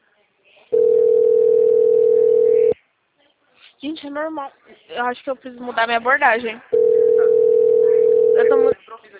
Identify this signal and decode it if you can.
Ringback tone (North America), 440 + 480 Hz, with speech.
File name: out- -2016- - - .57556